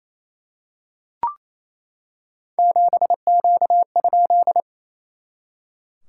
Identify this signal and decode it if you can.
Morse code, heard as a keyed tone, about 28 wpm, 690 Hz. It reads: 7Q?
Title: SOTA QSO Practice 28wpm